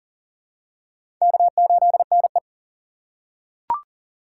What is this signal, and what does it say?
Morse code, heard as a keyed tone, about 40 wpm, 700 Hz. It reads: K8NE
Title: USA Call Signs - 4 Characters - 4x Character Spacing - 40wpm